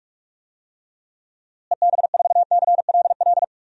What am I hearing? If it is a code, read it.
Morse code, heard as a keyed tone, about 44 wpm, 700 Hz. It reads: EB4CLL
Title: DXCC-100 Call Signs - 6 Characters - Repeated 3x - 45wpm